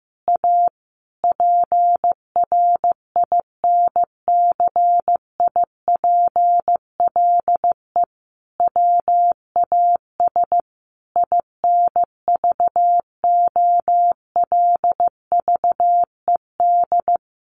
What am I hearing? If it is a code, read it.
Morse code, heard as a keyed tone, about 15 wpm, 700 Hz. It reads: A PRINCIPLE WAS INVOLVED
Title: Sentences from Top 1400 Words 15wpm